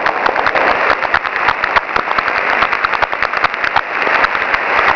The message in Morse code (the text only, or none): none